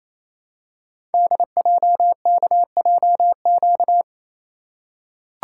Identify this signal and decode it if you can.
Morse code, heard as a keyed tone, about 27 wpm, 700 Hz. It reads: DJKJQ
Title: Code Groups - 5 Letters - 28wpm